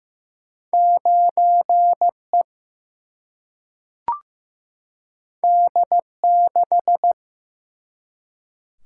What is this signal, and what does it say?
Morse code, heard as a keyed tone, about 15 wpm, 700 Hz. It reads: 9E D6
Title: Single-Letter Single-Number 15wpm